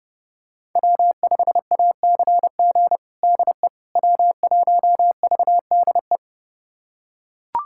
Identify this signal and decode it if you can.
Morse code, heard as a keyed tone, about 32 wpm, 700 Hz. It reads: W5ACG DE W1VDE
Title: USA Callsign DE Callsign - 5 Characters - 30wpm